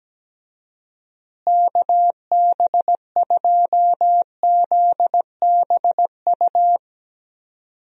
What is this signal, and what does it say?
Morse code, heard as a keyed tone, about 17 wpm, 700 Hz. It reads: KB2ZBU
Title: USA Call Signs - 6 Characters - Repeated 3x - 17wpm